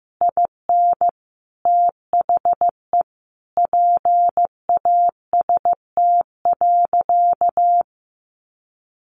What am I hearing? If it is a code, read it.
Morse code, heard as a keyed tone, about 15 wpm, 700 Hz. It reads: IN THE PAST.
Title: A Sound Of Thunder 15wpm